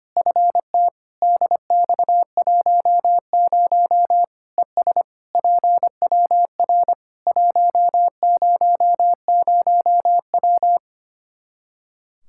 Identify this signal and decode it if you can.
Morse code, heard as a keyed tone, about 25 wpm, 700 Hz. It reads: FT DX10 ES PWR 100W
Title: Practice QSO - Exchange 2 - Info - 25wpm